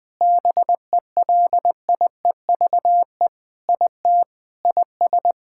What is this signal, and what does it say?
Morse code, heard as a keyed tone, about 20 wpm, 700 Hz. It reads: BELIEVE IT IS
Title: Sentences with 5-Word Phrases - Spoken Beginning - 20wpm